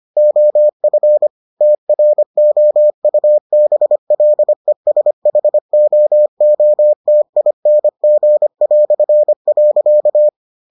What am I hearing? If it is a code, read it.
Morse code, heard as a keyed tone, about 25 wpm, 600 Hz. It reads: OF TROUBLESHOOTING".